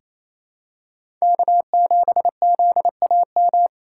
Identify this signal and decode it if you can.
Morse code, heard as a keyed tone, about 28 wpm, 700 Hz. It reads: K7ZAM
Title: USA Call Signs - 5 Characters - 4x Character Spacing - 28wpm